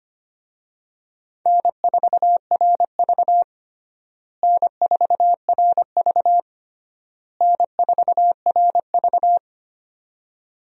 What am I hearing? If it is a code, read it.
Morse code, heard as a keyed tone, about 25 wpm, 700 Hz. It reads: N4RV N4RV N4RV
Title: USA Call Signs - 4 Characters - Repeated 3x - 25wpm